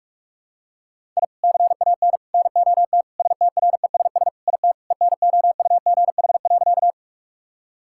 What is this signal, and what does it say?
Morse code, heard as a keyed tone, about 46 wpm, 700 Hz. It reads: I CAN NOT STRESS IT ENOUGH.
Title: Sentences from Top 1000 Words - Review Part 6 - 45wpm